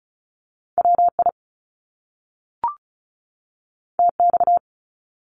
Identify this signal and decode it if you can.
Morse code, heard as a keyed tone, about 36 wpm, 700 Hz. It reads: WI TX
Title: US States 35wpm